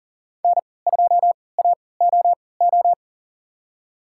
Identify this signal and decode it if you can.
Morse code, heard as a keyed tone, about 40 wpm, spaced as for 23 wpm, 700 Hz. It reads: N2AOO
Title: USA Call Signs - 5 Characters - 3x Character Spacing - 40wpm